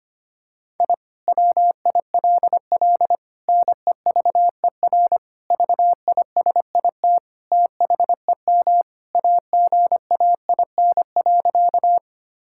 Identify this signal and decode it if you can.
Morse code, heard as a keyed tone, about 25 wpm, 700 Hz. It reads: I WILL NEVER VISIT THEM AGAIN.
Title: Sentences from Top 1000 Words - Review Part 6 - 25wpm